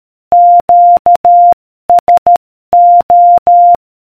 Morse code, heard as a keyed tone, about 13 wpm, 700 Hz. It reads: QSO